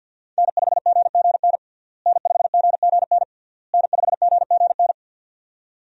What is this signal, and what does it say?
Morse code, heard as a keyed tone, about 52 wpm, 700 Hz. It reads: N5GGN N5GGN N5GGN
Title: USA Call Signs - 5 Characters - Repeated 3x - 50wpm